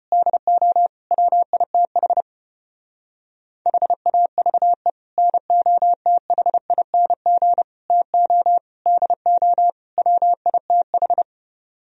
Morse code, heard as a keyed tone, about 33 wpm, 700 Hz. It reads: DO WITH HAVE NOTHING TO DO WITH